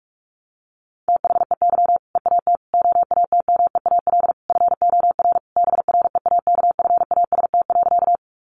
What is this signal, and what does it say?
Morse code, heard as a keyed tone, about 45 wpm, 700 Hz. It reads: THEY EAT OATMEAL FOR BREAKFAST.
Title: Easy Phrases 45wpm